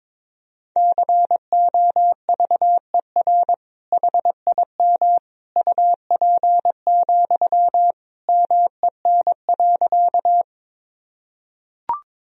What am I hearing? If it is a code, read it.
Morse code, heard as a keyed tone, about 22 wpm, 700 Hz. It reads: COVER HIM UP, MEN.
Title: Sentences from Top 1000 Words - Review Part 1 - 22wpm